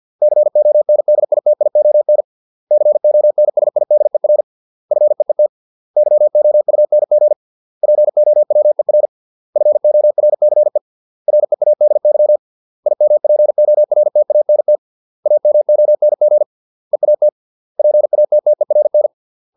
Morse code, heard as a keyed tone, about 50 wpm, 600 Hz. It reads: CONDITION CONSIDER FEET YOUNG POWER FORCE READY IMPORTANT AMONG EAT PATTERN